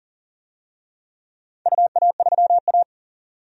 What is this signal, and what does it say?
Morse code, heard as a keyed tone, about 40 wpm, 700 Hz. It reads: UA3A
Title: DXCC-100 Call Signs - 4 Characters - 40wpm